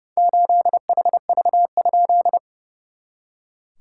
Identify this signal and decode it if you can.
Morse code, heard as a keyed tone, about 30 wpm, 690 Hz. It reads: 8HV?
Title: SOTA QSO Practice 30wpm